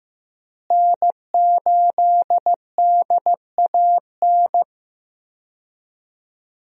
Morse code, heard as a keyed tone, about 15 wpm, 700 Hz. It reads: N8DAN
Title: USA Call Signs - 5 Characters - Repeated 2x - 15wpm